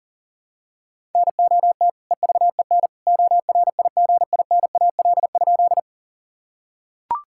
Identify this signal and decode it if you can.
Morse code, heard as a keyed tone, about 40 wpm, 700 Hz. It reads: NOT EVEN ORIGINAL?
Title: Sentences from Top 1000 Words - Review Part 6 - 40wpm